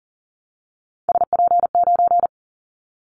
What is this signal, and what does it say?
Morse code, heard as a keyed tone, about 37 wpm, 700 Hz. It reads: SP9
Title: Callsign Prefixes for Top 25 DXCC - Repeated 3x - 40wpm